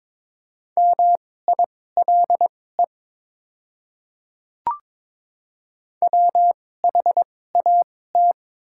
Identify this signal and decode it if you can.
Morse code, heard as a keyed tone, about 22 wpm, spaced as for 16 wpm, 700 Hz. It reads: MILE WHAT
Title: 2x Character Spacing - Top 300 Words - 22wpm